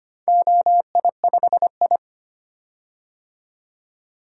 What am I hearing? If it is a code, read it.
Morse code, heard as a keyed tone, about 25 wpm, 700 Hz. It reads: OI5I
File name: practice / 25wpm / Four Letter-Number Combinations - 25wpm